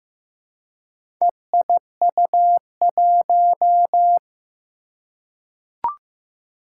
Morse code, heard as a keyed tone, about 15 wpm, 700 Hz. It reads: EIU1
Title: Four Challenging Letter-Number Combinations 15wpm